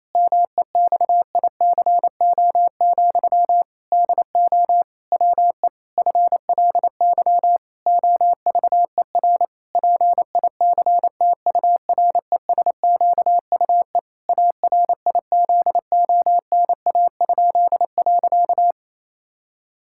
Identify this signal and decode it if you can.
Morse code, heard as a keyed tone, about 28 wpm, 700 Hz. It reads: MEXICO, DO WE FLY OVER PICTURESQUE ARIZONA?.
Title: Pangram Challenge v2 - 28wpm